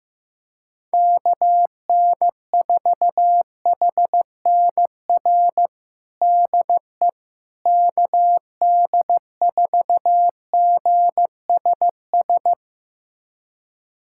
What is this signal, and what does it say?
Morse code, heard as a keyed tone, about 15 wpm, 700 Hz. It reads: KN4HNR DE KD4GSS
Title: USA Callsign DE Callsign - 6 Characters - 15wpm